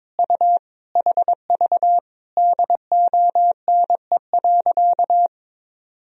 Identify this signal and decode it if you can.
Morse code, heard as a keyed tone, about 22 wpm, 700 Hz. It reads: U HV DONE.
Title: Sentences from Top 500 Words and Abbreviations 22wpm